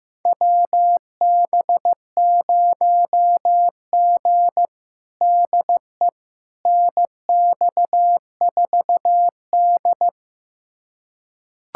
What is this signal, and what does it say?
Morse code, heard as a keyed tone, about 16 wpm, 700 Hz. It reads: WB0G DE NX4D
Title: USA Callsign DE Callsign - 4 Characters - 15wpm